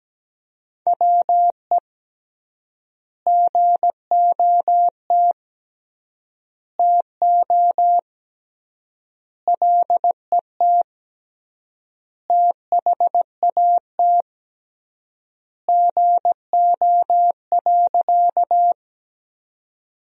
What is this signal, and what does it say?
Morse code, heard as a keyed tone, about 17 wpm, 700 Hz. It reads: WE GOT TO LET THAT GO.
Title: 3x Word-Spacing - Sentences from Top 300 Words - 17wpm